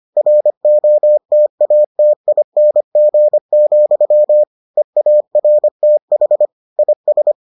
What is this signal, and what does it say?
Morse code, heard as a keyed tone, about 25 wpm, 600 Hz. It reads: ROTATING, EARTH IS